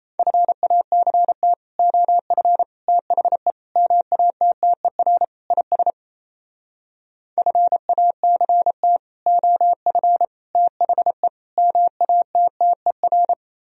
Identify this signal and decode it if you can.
Morse code, heard as a keyed tone, about 36 wpm, 700 Hz. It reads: FACT OF THE MATTER IS FACT OF THE MATTER